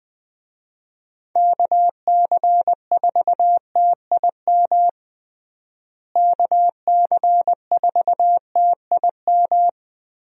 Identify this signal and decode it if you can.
Morse code, heard as a keyed tone, about 20 wpm, 700 Hz. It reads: KC4TIM KC4TIM
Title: USA Call Signs - 6 Characters - Repeated 3x - 20wpm